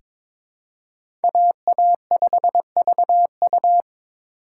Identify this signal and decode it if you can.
Morse code, heard as a keyed tone, about 22 wpm, 700 Hz. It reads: AA5VU
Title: K1USN SST QSO Practice - 22wpm